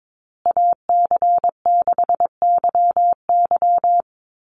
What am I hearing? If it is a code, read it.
Morse code, heard as a keyed tone, about 22 wpm, 700 Hz. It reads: AC6YY
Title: Call Signs Repeated 2x - 22wpm